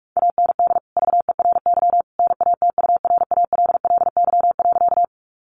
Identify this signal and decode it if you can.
Morse code, heard as a keyed tone, about 44 wpm, 700 Hz. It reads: AND VERY NATURALLY.
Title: Sentences from Top 1300 Words 45wpm